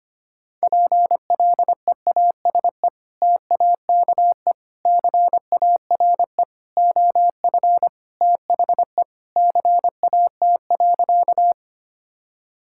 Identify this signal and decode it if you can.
Morse code, heard as a keyed tone, about 25 wpm, 700 Hz. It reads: PLEASE TAKE CARE OF THE CAT.
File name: practice / 25wpm / Sentences with 3-Word Phrases - 25wpm